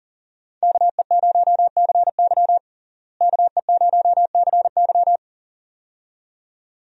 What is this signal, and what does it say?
Morse code, heard as a keyed tone, about 40 wpm, 700 Hz. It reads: KE0CY KE0CY
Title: USA Call Signs - 5 Characters - Repeated 2x - 40wpm